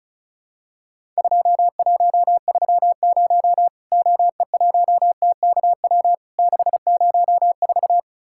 Morse code, heard as a keyed tone, about 36 wpm, 700 Hz. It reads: 2130 OE1TKW 604
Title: Worked All Europe DX Contest - QTC Exchange - 35wpm